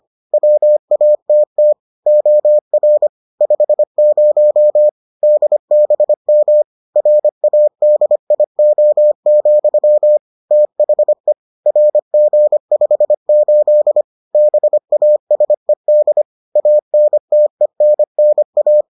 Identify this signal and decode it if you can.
Morse code, heard as a keyed tone, about 25 wpm, 600 Hz. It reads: WATT OR 50 DBM RADIO, THE RG58 BASED ANTENNA